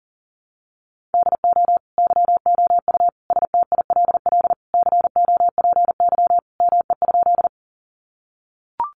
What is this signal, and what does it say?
Morse code, heard as a keyed tone, about 40 wpm, 700 Hz. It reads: DO YOU STILL COPY ME?
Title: Sentences from Top 1400 Words 40wpm